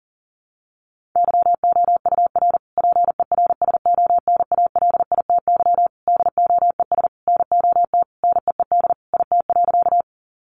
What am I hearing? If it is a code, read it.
Morse code, heard as a keyed tone, about 40 wpm, 700 Hz. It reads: YOUR PERSONALITY DOES NOT NEED IT.
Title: Sentences from Top 1400 Words 40wpm